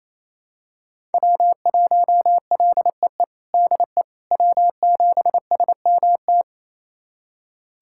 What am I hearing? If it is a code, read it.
Morse code, heard as a keyed tone, about 28 wpm, 700 Hz. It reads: W1LEE DE W7SMT
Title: USA Callsign DE Callsign - 5 Characters - 28wpm